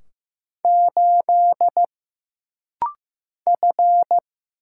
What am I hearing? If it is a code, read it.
Morse code, heard as a keyed tone, about 15 wpm, 700 Hz. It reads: 8 F
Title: Single Letter-Number Alt - Rapid-Fire 15wpm